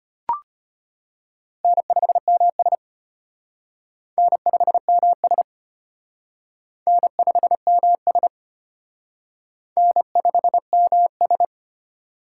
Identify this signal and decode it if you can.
Morse code, heard as a keyed tone, about 32 wpm, 700 Hz. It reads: N5MS N5MS N5MS N5MS